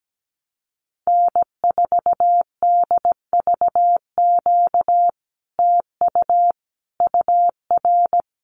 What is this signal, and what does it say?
Morse code, heard as a keyed tone, about 17 wpm, 700 Hz. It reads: N4DVQ TU UR
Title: Common QSO Phrases Encore 17wpm